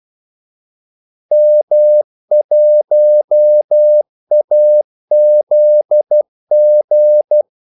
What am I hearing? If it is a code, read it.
Morse code, heard as a keyed tone, about 12 wpm, 600 Hz. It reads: M1AZG
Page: 2 ekstra ordmellemrum mellem kaldesignalerne